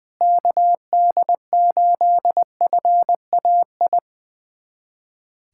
Morse code, heard as a keyed tone, about 20 wpm, 700 Hz. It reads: KD8FAI
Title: USA Callsign DE Callsign 20wpm